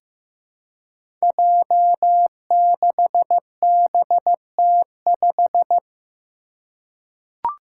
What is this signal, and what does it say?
Morse code, heard as a keyed tone, about 15 wpm, 700 Hz. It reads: J6BT5
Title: Five Challenging Letter-Number Combinations 15wpm